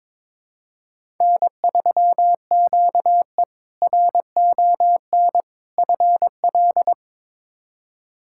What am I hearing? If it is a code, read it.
Morse code, heard as a keyed tone, about 22 wpm, 700 Hz. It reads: N3QE RON FL
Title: K1USN SST QSO Practice - 22wpm